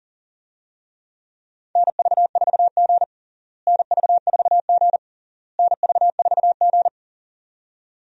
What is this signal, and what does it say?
Morse code, heard as a keyed tone, about 40 wpm, 700 Hz. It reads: NV4G NV4G NV4G
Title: USA Call Signs - 4 Characters - Repeated 3x - 40wpm